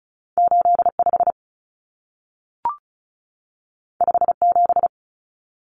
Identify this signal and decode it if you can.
Morse code, heard as a keyed tone, about 36 wpm, 700 Hz. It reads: 85 57
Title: Common Cut Numbers - 2-Digit 35wpm